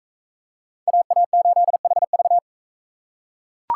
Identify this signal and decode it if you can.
Morse code, heard as a keyed tone, about 39 wpm, 700 Hz. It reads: AA8HV